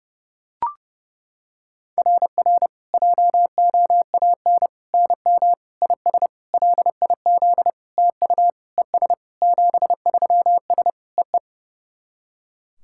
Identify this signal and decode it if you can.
Morse code, heard as a keyed tone, about 30 wpm, 690 Hz. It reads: RR JOAN NM IS LIZ TU ES 73S EE
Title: SOTA QSO Practice 30wpm